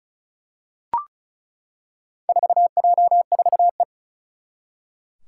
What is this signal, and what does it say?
Morse code, heard as a keyed tone, about 36 wpm, 700 Hz. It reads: 4J4E
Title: Four Challenging Letter-Number Combinations 35wpm